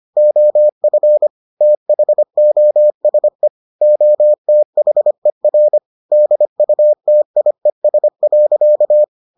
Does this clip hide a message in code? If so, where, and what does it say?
Yes: Morse code from 0.2 to 9.0 s, heard as a keyed tone, about 25 wpm, 600 Hz. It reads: OF THOSE OTHER DUTIES.